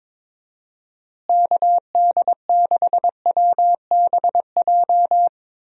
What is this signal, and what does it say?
Morse code, heard as a keyed tone, about 22 wpm, 700 Hz. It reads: KD6WBJ
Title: USA Call Signs - 6 Characters - Repeated 3x - 22wpm